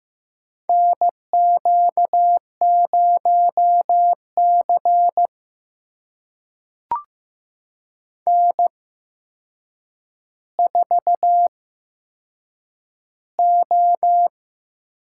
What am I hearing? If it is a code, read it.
Morse code, heard as a keyed tone, about 15 wpm, spaced as for 7 wpm, 700 Hz. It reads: NQ0C N4O
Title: USA Call Signs - 4 Characters - 8x Character Spacing - 15wpm